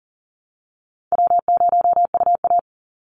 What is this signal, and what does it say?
Morse code, heard as a keyed tone, about 40 wpm, 700 Hz. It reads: W0UA
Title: Easy Call Signs Repeated 2x - 40wpm